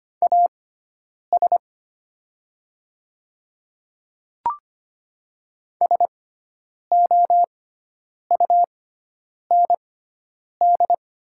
Morse code, heard as a keyed tone, about 25 wpm, spaced as for 9 wpm, 700 Hz. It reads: AS SOUND
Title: 6x Character Spacing - Top 100 Words - 25wpm